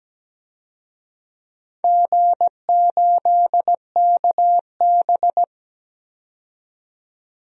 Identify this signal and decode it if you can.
Morse code, heard as a keyed tone, about 17 wpm, 700 Hz. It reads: G8KB
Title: DXCC-100 Call Signs - 4 Characters - 17wpm